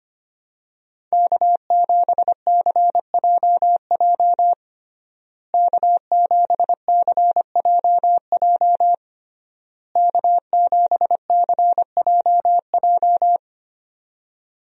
Morse code, heard as a keyed tone, about 25 wpm, 700 Hz. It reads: K7CJJ K7CJJ K7CJJ
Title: USA Call Signs - 5 Characters - Repeated 3x - 25wpm